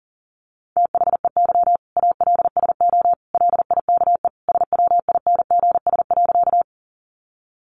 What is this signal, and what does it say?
Morse code, heard as a keyed tone, about 40 wpm, 700 Hz. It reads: THEY ALSO LIKE SWINGS.
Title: Easy Phrases 40wpm